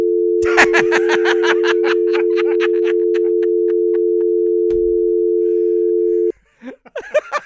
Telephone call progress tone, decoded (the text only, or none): dial tone